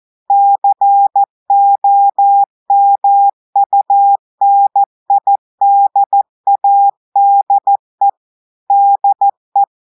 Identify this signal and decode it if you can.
Morse code, heard as a keyed tone, about 14 wpm, 800 Hz. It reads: COMUNIDADE DE
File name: audios / portugais / portugais texte en clair 1 vitesse 14 mots minutes